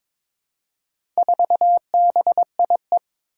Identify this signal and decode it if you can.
Morse code, heard as a keyed tone, about 22 wpm, 700 Hz. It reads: 4BIE
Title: Four Challenging Letter-Number Combinations 22wpm